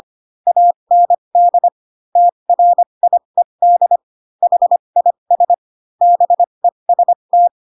Morse code, heard as a keyed tone, about 25 wpm, spaced as for 22 wpm, 700 Hz. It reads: AND TRIED HIS BEST
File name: Pinocchio / Pinocchio ch3 18 25 ik1hge